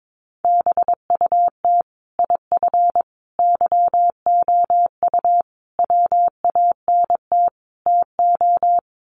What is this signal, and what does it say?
Morse code, heard as a keyed tone, about 22 wpm, 700 Hz. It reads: BUT IF YOU WANT TO
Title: Sets of 5 Words Encore2 22wpm